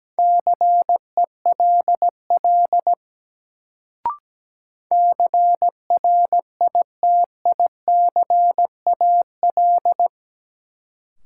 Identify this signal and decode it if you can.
Morse code, heard as a keyed tone, about 17 wpm, 700 Hz. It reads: CELL CRITICAL
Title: Sentences from Top 1100 Words - New Word Drill Alt - 17wpm